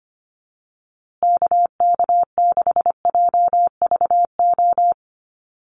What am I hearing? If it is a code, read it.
Morse code, heard as a keyed tone, about 25 wpm, 700 Hz. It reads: KK6JVO
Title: USA Call Signs Repeated 3x 25wpm